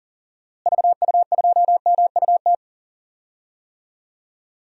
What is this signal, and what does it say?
Morse code, heard as a keyed tone, about 37 wpm, 700 Hz. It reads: VU2MUT
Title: DXCC-100 Call Signs - 6 Characters - Repeated 2x - 40wpm